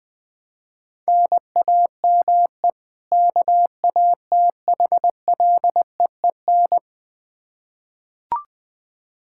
Morse code, heard as a keyed tone, about 20 wpm, 700 Hz. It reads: NAME KATHLEEN
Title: Practice QSO - Exchange 1 - Beginning - 20wpm